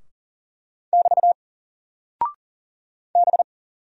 Morse code, heard as a keyed tone, about 40 wpm, 700 Hz. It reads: <BT> B
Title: Single Character (Letter:Number:Punctuation:Prosign) Alt 40wpm